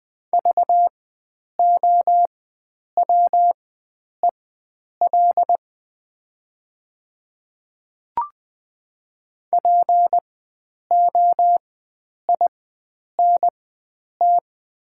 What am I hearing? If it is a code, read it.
Morse code, heard as a keyed tone, about 20 wpm, spaced as for 9 wpm, 700 Hz. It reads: VOWEL POINT
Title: 4x Character Spacing - Top 500 Words - 20wpm